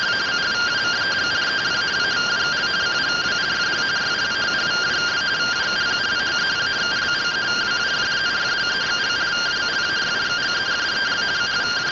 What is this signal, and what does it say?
Two-tone teleprinter signal, about 170 Hz shift.